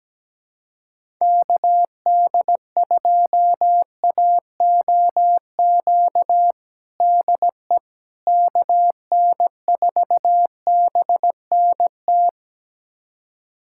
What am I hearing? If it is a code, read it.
Morse code, heard as a keyed tone, about 17 wpm, 700 Hz. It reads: KD2AOQ DE KN4BNT
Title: USA Callsign DE Callsign - 6 Characters - 17wpm